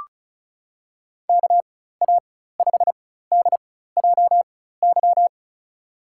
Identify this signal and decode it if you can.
Morse code, heard as a keyed tone, about 36 wpm, spaced as for 16 wpm, 700 Hz. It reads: KA5DJY
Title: USA Call Signs - 6 Characters - 4x Character Spacing - 35wpm